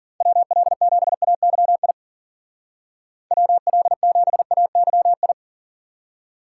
Morse code, heard as a keyed tone, about 49 wpm, 700 Hz. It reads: WL7AYI WL7AYI
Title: Speed Racing - USA Call Signs 40wpm